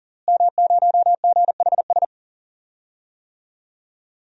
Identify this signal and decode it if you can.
Morse code, heard as a keyed tone, about 40 wpm, 700 Hz. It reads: M0GHS
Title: DXCC-100 Call Signs - 5 Characters - Repeated 2x - 40wpm